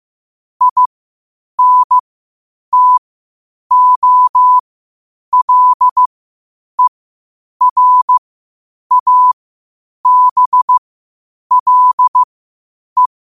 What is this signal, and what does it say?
Morse code, heard as a keyed tone, about 15 wpm, spaced as for 8 wpm, 1000 Hz. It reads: INTOLERABLE
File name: qotdfiles / 8wpm